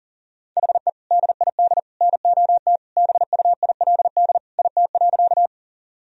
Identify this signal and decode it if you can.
Morse code, heard as a keyed tone, about 37 wpm, 700 Hz. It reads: HE DID NOT BUILD IT.
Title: Sentences from Top 1000 Words - Review Part 1 - 40wpm